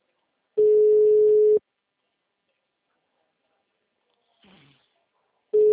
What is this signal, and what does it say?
Ringback tone (Europe, 425 Hz).
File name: out- -2019- - - .147